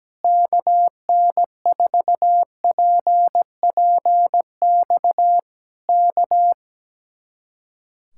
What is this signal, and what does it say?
Morse code, heard as a keyed tone, about 17 wpm, 700 Hz. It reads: KN4PPX K
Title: Speed-Racing Common QSO Phrases - 17wpm